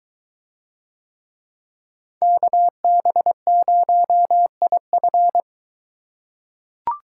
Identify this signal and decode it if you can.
Morse code, heard as a keyed tone, about 21 wpm, 700 Hz. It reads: KB0IF